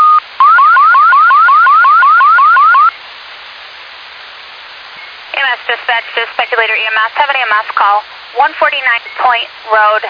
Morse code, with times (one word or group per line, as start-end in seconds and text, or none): none